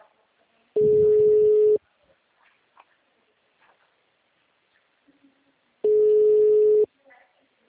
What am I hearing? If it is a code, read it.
Ringback tone (Europe, 425 Hz), with speech.
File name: out- -2012- - - .484